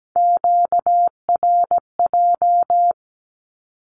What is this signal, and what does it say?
Morse code, heard as a keyed tone, about 17 wpm, 700 Hz. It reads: QRJ
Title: Q-Codes with Spoken Meaning - 17wpm